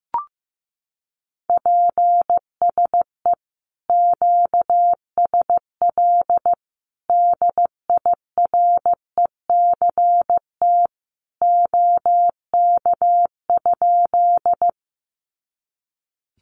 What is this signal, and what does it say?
Morse code, heard as a keyed tone, about 15 wpm, 700 Hz. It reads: PSE QSL DIRECT OK?
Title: Common QSO Phrases 15wpm